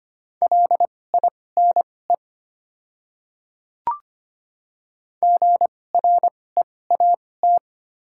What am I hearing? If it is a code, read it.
Morse code, heard as a keyed tone, about 25 wpm, spaced as for 18 wpm, 700 Hz. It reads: LINE GREAT
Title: 2x Character Spacing - Top 300 Words - 25wpm